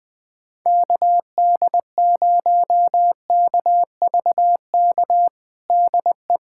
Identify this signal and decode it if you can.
Morse code, heard as a keyed tone, about 20 wpm, 700 Hz. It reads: KD0KVK DE
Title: USA Callsign DE Callsign - 6 Characters - 20wpm